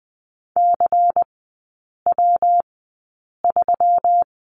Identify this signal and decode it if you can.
Morse code, heard as a keyed tone, about 20 wpm, spaced as for 8 wpm, 700 Hz. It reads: CW3
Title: 133 Lesson 22e - Question Mark - Three Characters 20wpm